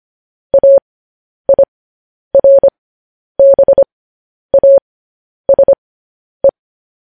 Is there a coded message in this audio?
Yes: Morse code from 0.5 to 6.5 s, heard as a keyed tone, about 25 wpm, spaced as for 10 wpm, 550 Hz. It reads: AIRBASE